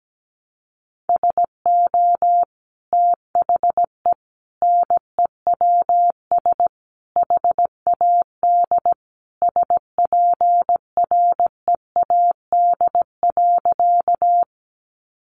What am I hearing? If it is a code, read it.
Morse code, heard as a keyed tone, about 17 wpm, 700 Hz. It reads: SO THE NEWS HAD SPREAD.
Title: Sentences from Top 1200 Words 17wpm